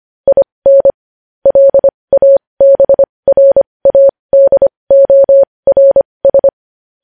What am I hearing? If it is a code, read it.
Morse code, heard as a keyed tone, about 25 wpm, spaced as for 20 wpm, 550 Hz. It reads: IN LABRADORS
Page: news-headlines-in-morse-code-at-20-wpm